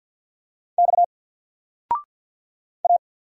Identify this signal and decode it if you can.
Morse code, heard as a keyed tone, about 48 wpm, 700 Hz. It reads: X A
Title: Single Character (Letter:Number:Punctuation:Prosign) 50wpm